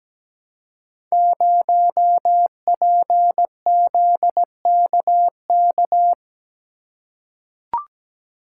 Morse code, heard as a keyed tone, about 17 wpm, 700 Hz. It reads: 0PZKK
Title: Code Groups - 5 Letter:Numbers - 17wpm